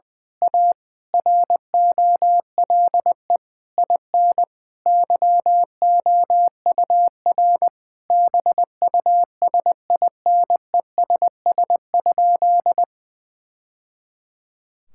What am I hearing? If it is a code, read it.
Morse code, heard as a keyed tone, about 20 wpm, 700 Hz. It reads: A ROLE IN YOUR BUSINESS?
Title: Sentences from Top 1000 Words - Review Part 6 - 20wpm